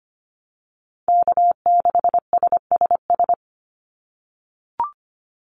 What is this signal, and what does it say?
Morse code, heard as a keyed tone, about 25 wpm, 700 Hz. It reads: K6SSS
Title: Call Signs - 25wpm